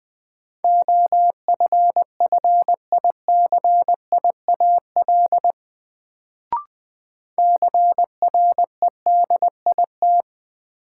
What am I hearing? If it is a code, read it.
Morse code, heard as a keyed tone, about 20 wpm, 700 Hz. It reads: OFFICIAL CREDIT
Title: Sentences from Top 1000 Words - New Word Drill - 20wpm